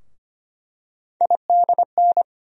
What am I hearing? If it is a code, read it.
Morse code, heard as a keyed tone, about 25 wpm, 700 Hz. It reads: IDN
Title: USA Call Sign Postfixes - 3 Letters 25wpm